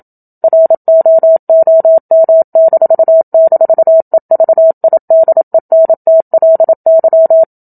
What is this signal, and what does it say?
Morse code, heard as a keyed tone, about 27 wpm, 650 Hz. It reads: ROOM--EVIDENTLY
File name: Book Two / 01-Chapter 1 - 27 WPM